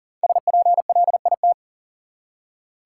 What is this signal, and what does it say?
Morse code, heard as a keyed tone, about 40 wpm, 700 Hz. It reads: SPLIT
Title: Five Letter Words - 40wpm